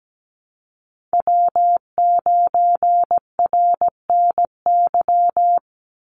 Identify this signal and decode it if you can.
Morse code, heard as a keyed tone, about 17 wpm, 700 Hz. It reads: W9RNY
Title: Call Signs - 17wpm